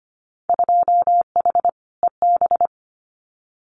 Morse code, heard as a keyed tone, about 25 wpm, 700 Hz. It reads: 2H EB
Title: Winter Field Day Exchanges Encore 25wpm